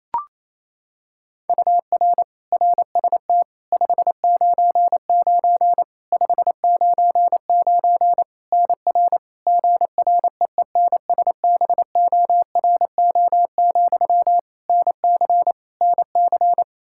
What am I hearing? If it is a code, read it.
Morse code, heard as a keyed tone, about 28 wpm, 700 Hz. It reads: UR RST 599 599 NR GREENSBORO, NC NC
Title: Practice QSO - Exchange 1 - Beginning - 28wpm